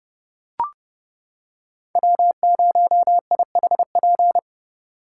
Morse code, heard as a keyed tone, about 28 wpm, 700 Hz. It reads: W0IHP